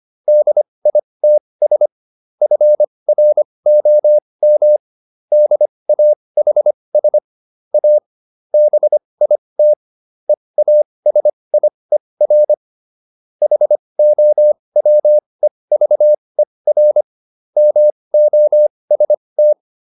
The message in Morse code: DITS FROM DAHS A BIT EASIER HOWEVER MOST

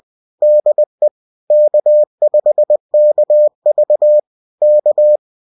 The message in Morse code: DE K5KV K

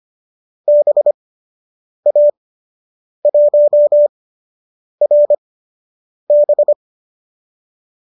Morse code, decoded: BA1RB

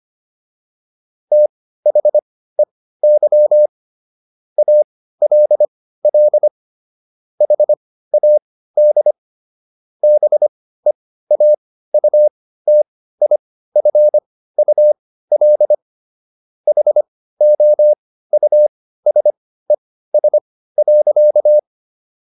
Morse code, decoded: THEY ALL HAD BEAUTIFUL HOUSES.

